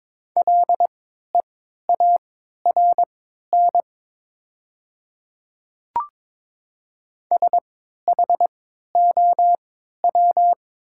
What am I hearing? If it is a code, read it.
Morse code, heard as a keyed tone, about 22 wpm, spaced as for 12 wpm, 690 Hz. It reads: LEARN SHOW
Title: 3x Character Spacing - Top 300 Words - 22wpm